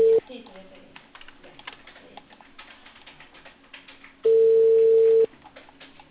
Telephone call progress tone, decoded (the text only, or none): ringback tone